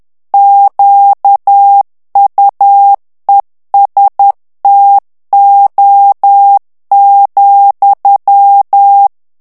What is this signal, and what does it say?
Morse code, heard as a keyed tone, about 10 wpm, 800 Hz. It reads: QUESTO,
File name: ITALIEN texte clair 2 vitesse 10 mots minute